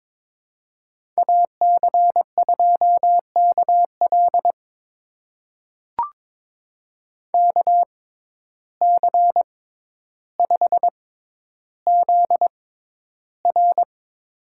Morse code, heard as a keyed tone, about 22 wpm, spaced as for 11 wpm, 700 Hz. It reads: AC2KL KC5ZR
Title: USA Call Signs - 5 Characters - 6x Character Spacing - 22wpm